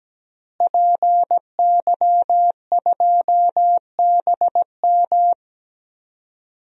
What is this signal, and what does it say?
Morse code, heard as a keyed tone, about 17 wpm, 700 Hz. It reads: PY2BM
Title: DXCC-100 Call Signs - 5 Characters - Repeated 3x - 17wpm